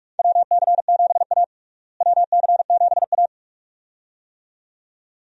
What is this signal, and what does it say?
Morse code, heard as a keyed tone, about 46 wpm, 700 Hz. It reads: WC7A WC7A
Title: USA Call Signs - 4 Characters - Repeated 2x - 45wpm